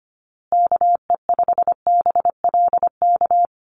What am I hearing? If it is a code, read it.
Morse code, heard as a keyed tone, about 25 wpm, 700 Hz. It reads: KE5BLK
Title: USA Call Signs Repeated 2x 25wpm